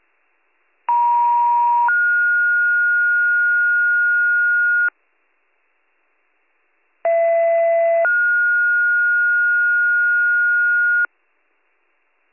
Two-tone page, heard as a one-second tone, then a three-second tone, about 950 then 1430 Hz, about 670 then 1430 Hz. It is heard from 0.9 to 11.1 s.